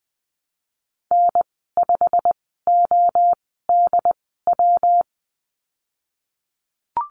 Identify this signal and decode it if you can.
Morse code, heard as a keyed tone, about 20 wpm, spaced as for 15 wpm, 700 Hz. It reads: N5ODW
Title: 186 Lesson 30g - Letter b - Call Signs 20wpm